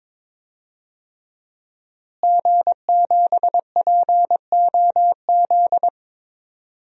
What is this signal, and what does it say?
Morse code, heard as a keyed tone, about 22 wpm, 700 Hz. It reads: G7POZ
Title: DXCC-100 Call Signs - 5 Characters - Repeated 3x - 22wpm